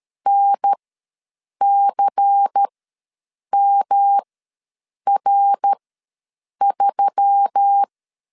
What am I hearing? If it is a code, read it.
Morse code, heard as a keyed tone, about 13 wpm, spaced as for 7 wpm, 800 Hz. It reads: NCMR3
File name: LEZIONI / 6 PKX0 / 6 PK ws 12wpm 3116